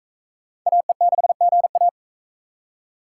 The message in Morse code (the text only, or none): AE6GA